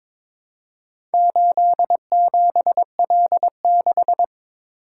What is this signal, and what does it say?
Morse code, heard as a keyed tone, about 22 wpm, 700 Hz. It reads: 87L6
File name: practice / 22wpm / Four Letter-Number Combinations - 22wpm